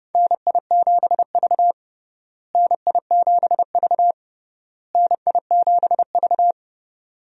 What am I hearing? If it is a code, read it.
Morse code, heard as a keyed tone, about 30 wpm, 700 Hz. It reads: NI7V NI7V NI7V
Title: USA Call Signs - 4 Characters - Repeated 3x - 30wpm